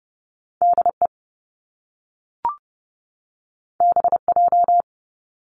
Morse code, heard as a keyed tone, about 30 wpm, 700 Hz. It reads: DE BJ
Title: Top Two Letter Combinations in Words 30wpm